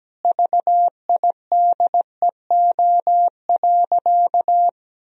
Morse code, heard as a keyed tone, about 17 wpm, 700 Hz. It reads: VIDEO.
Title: Sentences from Top 1000 Words - Review Part 4 - 17wpm